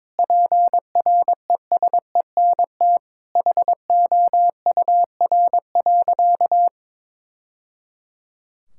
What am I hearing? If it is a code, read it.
Morse code, heard as a keyed tone, about 23 wpm, 700 Hz. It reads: PRESENT HOUR.
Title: Sentences from Top 1000 Words - Review Part 5 - 22wpm